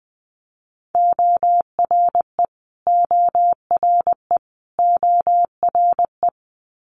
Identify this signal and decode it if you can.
Morse code, heard as a keyed tone, about 20 wpm, 700 Hz. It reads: ORE ORE ORE
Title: Multi-letter Phonemes Repeated 3x 20wpm